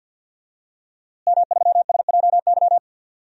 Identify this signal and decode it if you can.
Morse code, heard as a keyed tone, about 51 wpm, 700 Hz. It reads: M3SJY
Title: DXCC-100 Call Signs - 5 Characters - Repeated 3x - 50wpm